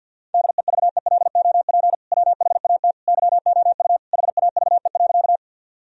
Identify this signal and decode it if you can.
Morse code, heard as a keyed tone, about 51 wpm, 700 Hz. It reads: DEVELOP WHAT YOU HAVE.
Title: Sentences from Top 1000 Words - Review Part 2 - 50wpm